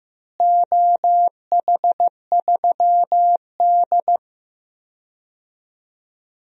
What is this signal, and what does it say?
Morse code, heard as a keyed tone, about 15 wpm, 700 Hz. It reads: OH3D
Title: DXCC-100 Call Signs - 4 Characters - Repeated 2x - 15wpm